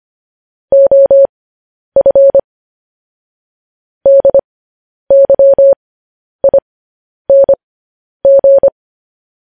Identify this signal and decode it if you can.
Morse code, heard as a keyed tone, about 25 wpm, spaced as for 10 wpm, 550 Hz. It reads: OF DYING